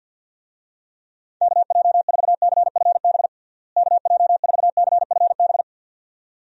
Morse code, heard as a keyed tone, about 51 wpm, 700 Hz. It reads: KJ4CFB KJ4CFB
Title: USA Call Signs - 6 Characters - Repeated 2x - 50wpm